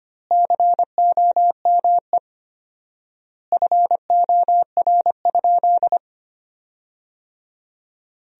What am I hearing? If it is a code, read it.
Morse code, heard as a keyed tone, about 25 wpm, 700 Hz. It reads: COME FOR?
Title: 4x Word-Spacing - Sentences from Top 100 Words - 25wpm